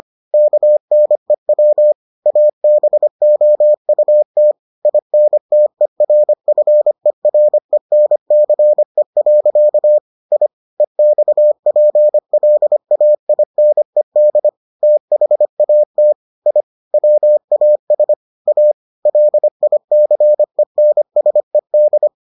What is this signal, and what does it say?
Morse code, heard as a keyed tone, about 25 wpm, 600 Hz. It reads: KNEW ABOUT INTERFERENCE. I EXPLAINED THAT I WAS A LICENSED